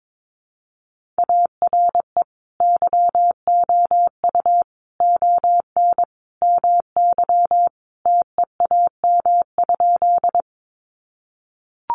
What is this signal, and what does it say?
Morse code, heard as a keyed tone, about 22 wpm, 700 Hz. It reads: ARE YOU ON MY TEAM?
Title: Sentences from Top 1500 Words 22wpm